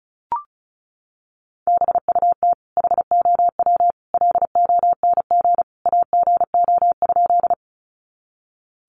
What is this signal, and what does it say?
Morse code, heard as a keyed tone, about 34 wpm, 700 Hz. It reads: BUT HOW LONG AGO?
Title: Sentences from Top 500 Words 35wpm